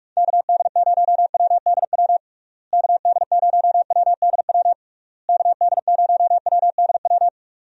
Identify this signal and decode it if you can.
Morse code, heard as a keyed tone, about 46 wpm, 700 Hz. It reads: KD0WDW KD0WDW KD0WDW
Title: USA Call Signs - 6 Characters - Repeated 3x - 45wpm